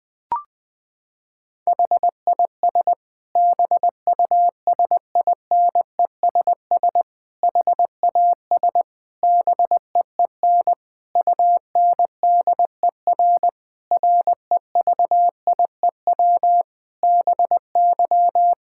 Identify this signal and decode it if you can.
Morse code, heard as a keyed tone, about 20 wpm, 700 Hz. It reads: HIS BUSINESS HAS BEEN UNDER REVIEW BY
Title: Sentences from Top 1000 Words - Review Part 6 - 20wpm